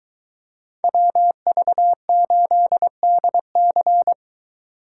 Morse code, heard as a keyed tone, about 22 wpm, 700 Hz. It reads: WV8DC